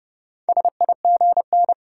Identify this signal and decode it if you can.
Morse code, heard as a keyed tone, about 30 wpm, 700 Hz. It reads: SIGN
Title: Sentences from Top 1000 Words - New Word Drill - 30wpm